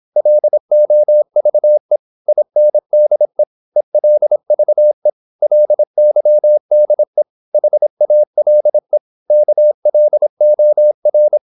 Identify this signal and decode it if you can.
Morse code, heard as a keyed tone, about 26 wpm, 600 Hz. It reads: LOVE INDE ELVE LYDE HALE KLOR
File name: Korte ord DK 26wpm